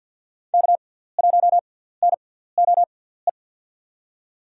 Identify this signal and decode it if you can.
Morse code, heard as a keyed tone, about 52 wpm, spaced as for 17 wpm, 700 Hz. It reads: K1NOE
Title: USA Call Signs - 5 Characters - 6x Character Spacing - 50wpm